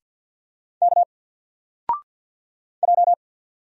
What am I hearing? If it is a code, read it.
Morse code, heard as a keyed tone, about 49 wpm, 700 Hz. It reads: K J
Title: Single Character (Letter:Number:Punctuation:Prosign) 50wpm